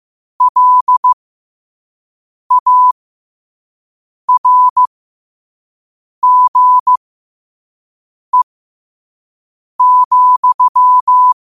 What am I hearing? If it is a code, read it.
Morse code, heard as a keyed tone, about 15 wpm, spaced as for 5 wpm, 1000 Hz. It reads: LARGE,
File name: 5wpm